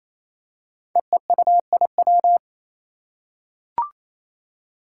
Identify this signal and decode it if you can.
Morse code, heard as a keyed tone, about 28 wpm, 700 Hz. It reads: EEUIW
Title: Five Challenging Letter-Number Combinations 28wpm